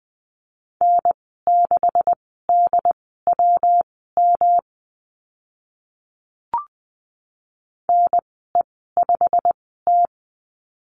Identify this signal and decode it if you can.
Morse code, heard as a keyed tone, about 20 wpm, spaced as for 14 wpm, 700 Hz. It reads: N6DWM NE5T
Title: 161 Lesson 26g - Letter g - Call Signs 20wpm